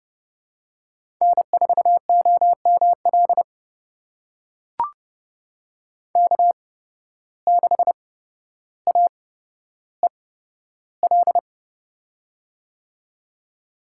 Morse code, heard as a keyed tone, about 29 wpm, spaced as for 13 wpm, 700 Hz. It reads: N4OML K6AEL
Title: USA Call Signs - 5 Characters - 8x Character Spacing - 30wpm